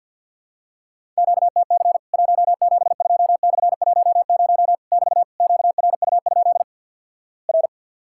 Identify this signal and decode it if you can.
Morse code, heard as a keyed tone, about 50 wpm, 700 Hz. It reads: QTC 172/10 <BT> QRU? R
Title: Worked All Europe DX Contest - QTC Exchange with Repeats - 50wpm